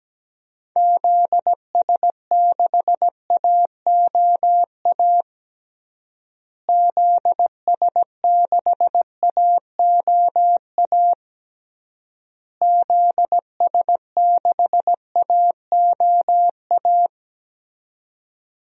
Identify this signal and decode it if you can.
Morse code, heard as a keyed tone, about 17 wpm, 700 Hz. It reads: ZS6AOA ZS6AOA ZS6AOA
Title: DXCC-100 Call Signs - 6 Characters - Repeated 3x - 17wpm